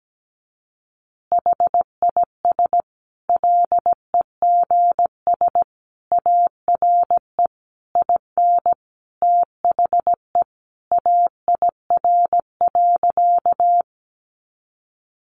Morse code, heard as a keyed tone, about 17 wpm, 700 Hz. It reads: HIS LEGS ARE IN THE AIR.
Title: Additional Easy Phrases 17wpm